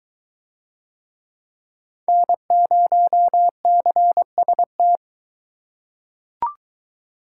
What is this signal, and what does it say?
Morse code, heard as a keyed tone, about 21 wpm, 700 Hz. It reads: N0CST